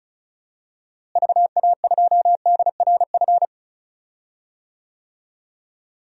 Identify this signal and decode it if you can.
Morse code, heard as a keyed tone, about 34 wpm, 700 Hz. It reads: VA2DRF
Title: DXCC-100 Call Signs - 6 Characters - Repeated 3x - 35wpm